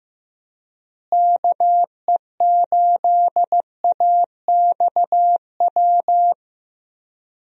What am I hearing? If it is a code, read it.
Morse code, heard as a keyed tone, about 15 wpm, 700 Hz. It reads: KE8AXW
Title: USA Call Signs - 6 Characters - Repeated 3x - 15wpm